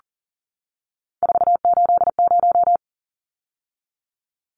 Morse code, heard as a keyed tone, about 40 wpm, 700 Hz. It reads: 480
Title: Numbers - 3-Digit 40wpm